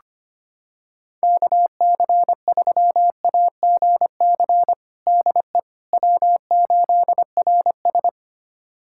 Morse code, heard as a keyed tone, about 25 wpm, 700 Hz. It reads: KC3AGC DE W8RS
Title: USA Callsign DE Callsign 25wpm